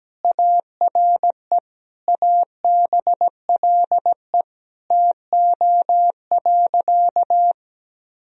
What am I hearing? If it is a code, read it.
Morse code, heard as a keyed tone, about 17 wpm, 700 Hz. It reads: ARE ABLE TO.
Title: Sentences from Top 1000 Words - Review Part 2 - 17wpm